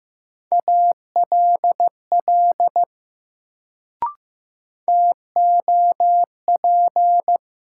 Morse code, heard as a keyed tone, about 15 wpm, 700 Hz. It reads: ALL TOP
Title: Sets of 2 Words - New Word Drill - 15wpm